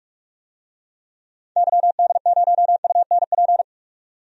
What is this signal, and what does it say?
Morse code, heard as a keyed tone, about 46 wpm, 700 Hz. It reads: YD0UNP
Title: DXCC-100 Call Signs - 6 Characters - 45wpm